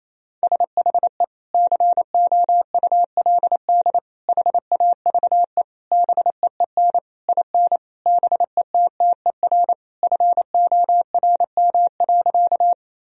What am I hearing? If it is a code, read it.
Morse code, heard as a keyed tone, about 28 wpm, 700 Hz. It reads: SHE COULD HAVE BEEN IN BETTER FORM.
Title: Sentences from Top 1000 Words - Review Part 1 - 28wpm